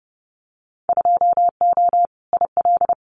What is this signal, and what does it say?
Morse code, heard as a keyed tone, about 30 wpm, 700 Hz. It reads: 2O IL
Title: Winter Field Day Exchanges 30wpm